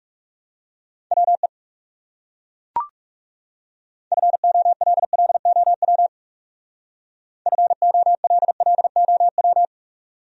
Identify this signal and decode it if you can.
Morse code, heard as a keyed tone, about 38 wpm, 700 Hz. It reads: WE FOLLOW FOLLOW
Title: Speed-Racing Top 200 Words 30wpm